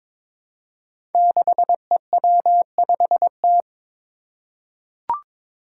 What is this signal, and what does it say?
Morse code, heard as a keyed tone, about 22 wpm, 700 Hz. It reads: 6EW5T
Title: Five Challenging Letter-Number Combinations 22wpm